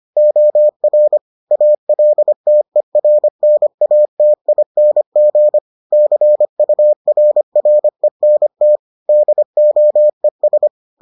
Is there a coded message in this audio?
Yes: Morse code from 0.2 to 10.7 s, heard as a keyed tone, about 25 wpm, 600 Hz. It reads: OR ALTERNATING CURRENT DOES